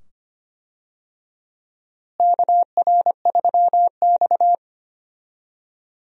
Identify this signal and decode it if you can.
Morse code, heard as a keyed tone, about 25 wpm, 700 Hz. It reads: KR3X
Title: USA Call Signs - 4 Characters - Repeated 2x - 25wpm